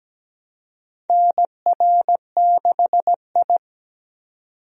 Morse code, heard as a keyed tone, about 17 wpm, 700 Hz. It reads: NR6I
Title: USA Call Signs - 4 Characters - 17wpm